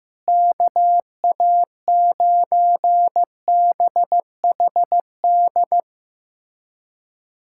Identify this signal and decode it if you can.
Morse code, heard as a keyed tone, about 16 wpm, 700 Hz. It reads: KA9BHD
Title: K1USN SST QSO Practice - 15wpm